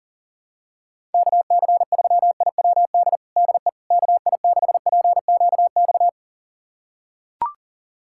Morse code, heard as a keyed tone, about 40 wpm, 700 Hz. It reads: KC3IWD DE KI6PQX
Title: USA Callsign DE Callsign - 6 Characters - 40wpm